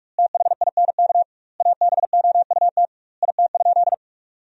Morse code, heard as a keyed tone, about 46 wpm, 700 Hz. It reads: THINK ABOUT IT?
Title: 2x Word-Spacing - Sentences from Top 200 Words - 45wpm